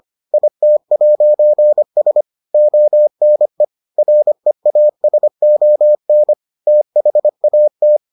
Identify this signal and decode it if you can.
Morse code, heard as a keyed tone, about 25 wpm, 600 Hz. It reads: IT'S ONE REASON THAT